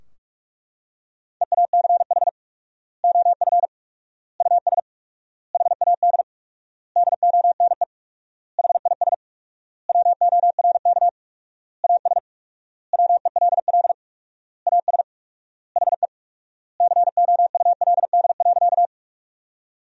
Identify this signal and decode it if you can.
Morse code, heard as a keyed tone, about 42 wpm, 700 Hz. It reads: EACH OF US HAD DONE HIS WORK AS WELL AS HE COULD.
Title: 4x Word-Spacing - Sentences from Top 500 Words - 45wpm